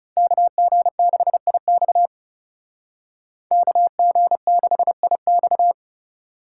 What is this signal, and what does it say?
Morse code, heard as a keyed tone, about 36 wpm, 700 Hz. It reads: KG6IX KG6IX
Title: Speed Racing - USA 2x2 Call Signs 30wpm